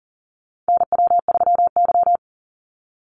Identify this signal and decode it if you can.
Morse code, heard as a keyed tone, about 40 wpm, 700 Hz. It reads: NW3Y
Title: Call Signs - 40wpm